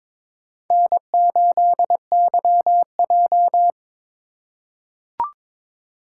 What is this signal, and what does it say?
Morse code, heard as a keyed tone, about 22 wpm, 700 Hz. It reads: N8YJ
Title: USA Call Signs - 4 Characters - 3x Character Spacing - 22wpm